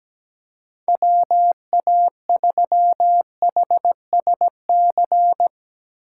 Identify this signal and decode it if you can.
Morse code, heard as a keyed tone, about 18 wpm, 700 Hz. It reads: WA3HSC
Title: Speed Racing - USA 2x3 Call Signs 17wpm